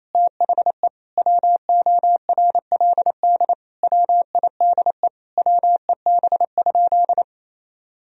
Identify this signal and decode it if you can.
Morse code, heard as a keyed tone, about 28 wpm, 700 Hz. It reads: THE WORLD WIDE WEB?
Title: Sentences from Top 1000 Words - Review Part 4 - 28wpm